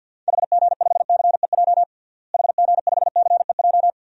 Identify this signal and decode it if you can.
Morse code, heard as a keyed tone, about 46 wpm, 700 Hz. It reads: HG5CEJ HG5CEJ
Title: DXCC-100 Call Signs - 6 Characters - Repeated 3x - 50wpm